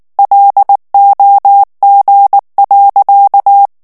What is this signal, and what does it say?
Morse code, heard as a keyed tone, about 19 wpm, 800 Hz. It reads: LOG.
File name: morse mp3 / ITALIEN texte clair 2 vitesse 18 mots minute